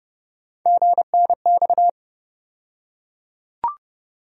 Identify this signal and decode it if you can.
Morse code, heard as a keyed tone, about 31 wpm, 700 Hz. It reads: GNX
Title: USA Call Sign Postfixes - 3 Letters 30wpm